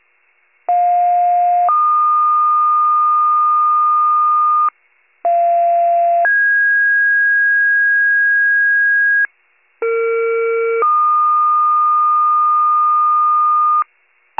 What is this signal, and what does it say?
Two-tone page, heard as a one-second tone, then a three-second tone, about 710 then 1220 Hz, about 690 then 1650 Hz, about 460 then 1220 Hz. It is heard from 0.7 to 13.8 s.